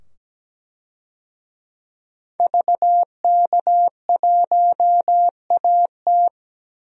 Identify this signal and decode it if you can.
Morse code, heard as a keyed tone, about 17 wpm, 700 Hz. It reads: VK1AT
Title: DXCC-100 Call Signs - 5 Characters - 17wpm